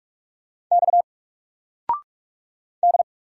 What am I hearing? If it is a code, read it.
Morse code, heard as a keyed tone, about 44 wpm, 700 Hz. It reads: X D
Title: Single Character (Letter:Number:Punctuation:Prosign) 45wpm